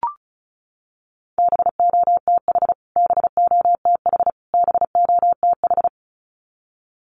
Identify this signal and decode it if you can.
Morse code, heard as a keyed tone, about 34 wpm, 700 Hz. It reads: BOTH BOTH BOTH
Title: Top 300 Words Repeated 35wpm